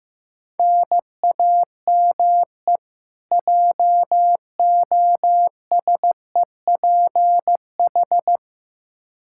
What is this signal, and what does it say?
Morse code, heard as a keyed tone, about 15 wpm, 700 Hz. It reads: NAME JOSEPH
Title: Practice QSO - Exchange 1 - Beginning - 15wpm